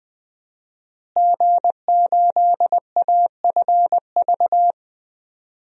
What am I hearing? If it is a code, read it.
Morse code, heard as a keyed tone, about 21 wpm, 700 Hz. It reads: G8AFV
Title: DXCC-100 Call Signs - 5 Characters - Repeated 2x - 20wpm